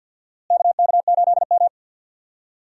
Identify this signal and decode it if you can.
Morse code, heard as a keyed tone, about 52 wpm, 700 Hz. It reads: KK8M
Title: USA Call Signs - 4 Characters - Repeated 3x - 50wpm